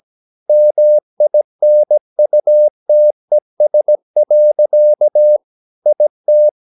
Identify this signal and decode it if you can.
Morse code, heard as a keyed tone, about 17 wpm, 600 Hz. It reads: MINUTES. IT
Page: Romanen Never af Ken Follet på engelsk.